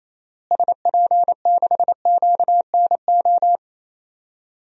Morse code, heard as a keyed tone, about 28 wpm, 700 Hz. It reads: SP6QNO
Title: DXCC-100 Call Signs - 6 Characters - Repeated 3x - 28wpm